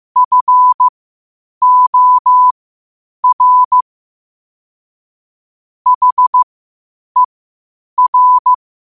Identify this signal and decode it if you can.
Morse code, heard as a keyed tone, about 15 wpm, spaced as for 8 wpm, 1000 Hz. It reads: FOR HER
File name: qotdfiles / 8wpm